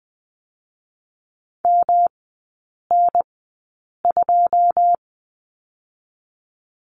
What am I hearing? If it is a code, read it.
Morse code, heard as a keyed tone, about 20 wpm, spaced as for 8 wpm, 700 Hz. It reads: MN2
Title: 107 Lesson 18e - Letter m - Three Characters 20wpm